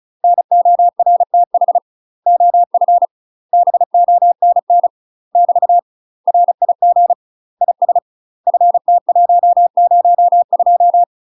Morse code, heard as a keyed tone, about 35 wpm, 700 Hz. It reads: NORTH OF BONN <BT> RIG IS FT102
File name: UK MASTER QSOs TRAINING / 35WPM / UK MASTER QSOs 0004